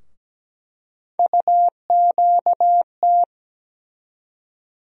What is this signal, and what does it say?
Morse code, heard as a keyed tone, about 17 wpm, 700 Hz. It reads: UQT
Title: USA Call Sign Postfixes - 3 Letters 17wpm